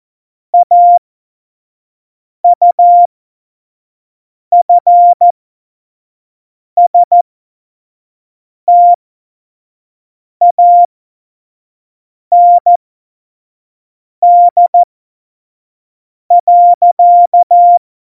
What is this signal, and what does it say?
Morse code, heard as a keyed tone, about 14 wpm, spaced as for 5 wpm, 700 Hz. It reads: AUFSTAND.
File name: Morsekurs / Witze / Witz 11 Juli 17, Tempo 25 F70